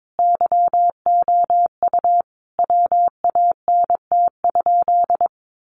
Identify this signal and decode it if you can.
Morse code, heard as a keyed tone, about 22 wpm, 700 Hz. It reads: YOU WANT?
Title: Sentences from Top 800 Words 22wpm